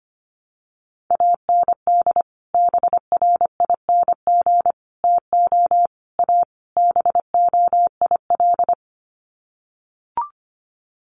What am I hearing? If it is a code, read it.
Morse code, heard as a keyed tone, about 25 wpm, 700 Hz. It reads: AND BRING TO A BOIL
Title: Sets of 5 Words Encore 25wpm